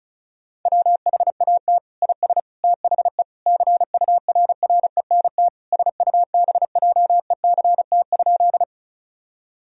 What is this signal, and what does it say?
Morse code, heard as a keyed tone, about 34 wpm, 700 Hz. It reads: WHAT IS THE CURRENT SUBJECT?
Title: Sentences from Top 1000 Words - Review Part 4 - 35wpm